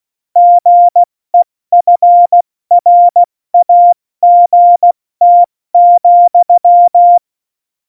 Morse code, heard as a keyed tone, about 16 wpm, 700 Hz. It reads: GEFRAGT,
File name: Morsekurs / Witze / Witz 09. Juli 2019, 70 F80